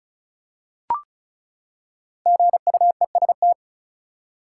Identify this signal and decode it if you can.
Morse code, heard as a keyed tone, about 34 wpm, 700 Hz. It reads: GUEST
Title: Five Letter Words - 35wpm